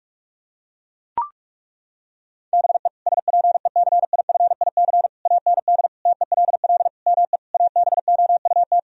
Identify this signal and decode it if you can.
Morse code, heard as a keyed tone, about 41 wpm, 700 Hz. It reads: BE SPECIFIC AND TELL ME ABOUT
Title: Sentences from Top 1000 Words - Review Part 3 - 45wpm